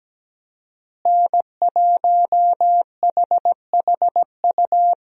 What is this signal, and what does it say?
Morse code, heard as a keyed tone, about 17 wpm, 700 Hz. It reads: N1HHU
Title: USA Call Signs - 5 Characters - Repeated 2x - 17wpm